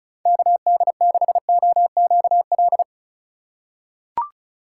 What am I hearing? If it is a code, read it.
Morse code, heard as a keyed tone, about 34 wpm, 700 Hz. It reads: KD6OQL
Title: USA Call Signs - 4x Character Spacing - 35wpm